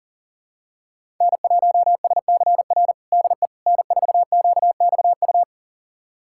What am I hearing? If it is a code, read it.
Morse code, heard as a keyed tone, about 40 wpm, 690 Hz. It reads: N1SCR DE N4QXU
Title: USA Callsign DE Callsign - 5 Characters - 40wpm